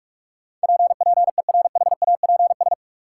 Morse code, heard as a keyed tone, about 46 wpm, 700 Hz. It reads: P PERHAPS
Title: Sentences from Top 1100 Words - New Word Drill Alt - 45wpm